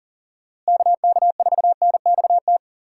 Morse code, heard as a keyed tone, about 40 wpm, 700 Hz. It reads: KK4NXT
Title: USA Call Signs - 6 Characters - Repeated 2x - 40wpm